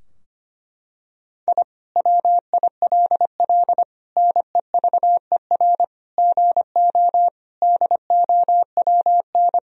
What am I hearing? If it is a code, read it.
Morse code, heard as a keyed tone, about 25 wpm, 700 Hz. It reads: I WILL NEVER GO DOWN
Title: Sentences from Top 1000 Words - Review Part 2 - 25wpm